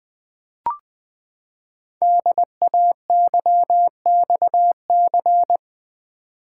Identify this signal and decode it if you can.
Morse code, heard as a keyed tone, about 20 wpm, 700 Hz. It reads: DAYXC
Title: Code Groups - 5 Letters - 20wpm